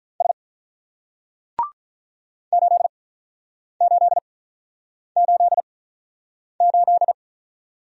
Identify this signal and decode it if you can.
Morse code, heard as a keyed tone, about 47 wpm, spaced as for 9 wpm, 700 Hz. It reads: S 8888